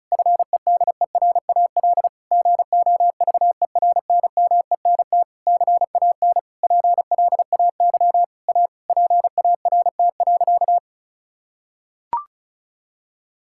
Morse code, heard as a keyed tone, about 36 wpm, 700 Hz. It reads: FEDERAL GOVERNMENT CAN PLAY A PART.
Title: Sentences from Top 1000 Words - Review Part 5 - 35wpm